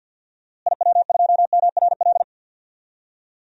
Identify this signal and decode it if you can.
Morse code, heard as a keyed tone, about 51 wpm, 700 Hz. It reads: IW2MRL
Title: DXCC-100 Call Signs - 6 Characters - Repeated 3x - 50wpm